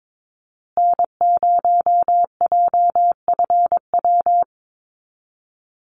Morse code, heard as a keyed tone, about 22 wpm, 700 Hz. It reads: N0JFW
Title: USA Call Signs Repeated 2x 22wpm